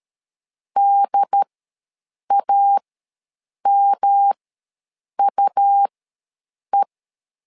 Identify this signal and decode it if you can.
Morse code, heard as a keyed tone, about 13 wpm, spaced as for 7 wpm, 800 Hz. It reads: DAMUE
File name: LEZIONI / 3 MWRO / 3 M ws 12wpm 1745